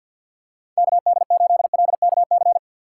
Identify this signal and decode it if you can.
Morse code, heard as a keyed tone, about 52 wpm, 700 Hz. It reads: KD8LKC
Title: USA Call Signs - 6 Characters - 3x Character Spacing - 50wpm